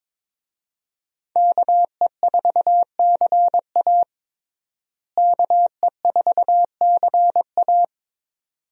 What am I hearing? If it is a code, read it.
Morse code, heard as a keyed tone, about 22 wpm, 700 Hz. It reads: KE4CA KE4CA
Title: USA Call Signs - 5 Characters - Repeated 2x - 22wpm